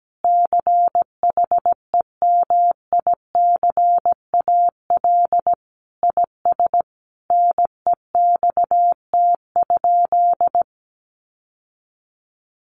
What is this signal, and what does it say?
Morse code, heard as a keyed tone, about 17 wpm, 700 Hz. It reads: CHEMICAL IS NEXT?
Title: Sentences from Top 1200 Words 17wpm